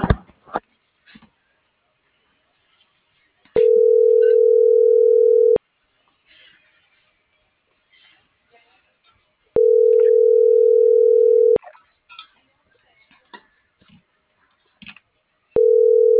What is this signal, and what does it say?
Ringback tone (North America), 440 + 480 Hz.